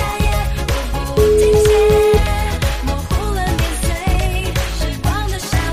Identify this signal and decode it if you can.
Ringback tone (Europe, 425 Hz), with noise.